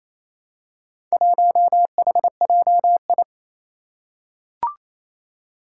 Morse code, heard as a keyed tone, about 27 wpm, 700 Hz. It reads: 1HJI
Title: Four Challenging Letter-Number Combinations 28wpm